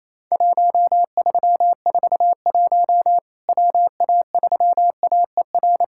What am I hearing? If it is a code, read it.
Morse code, heard as a keyed tone, about 28 wpm, 700 Hz. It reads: 1341 WA3AER
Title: Worked All Europe DX Contest - QTC Exchange - 28wpm